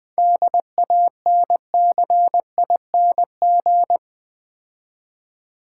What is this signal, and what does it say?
Morse code, heard as a keyed tone, about 21 wpm, 700 Hz. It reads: DANCING
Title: 7 Letter Words - First 3 Letters Given - 20wpm